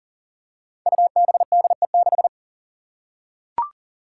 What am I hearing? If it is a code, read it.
Morse code, heard as a keyed tone, about 40 wpm, 700 Hz. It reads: UBDE6
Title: Five Challenging Letter-Number Combinations 40wpm